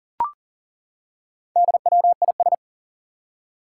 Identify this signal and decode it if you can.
Morse code, heard as a keyed tone, about 42 wpm, 700 Hz. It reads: DWIS
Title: Four Challenging Letter-Number Combinations 40wpm